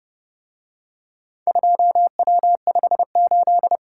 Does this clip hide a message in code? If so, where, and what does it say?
Yes: Morse code from 1.5 to 3.8 s, heard as a keyed tone, about 29 wpm, 700 Hz. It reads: 2W58